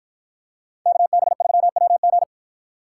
Morse code, heard as a keyed tone, about 48 wpm, 700 Hz. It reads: KB3WG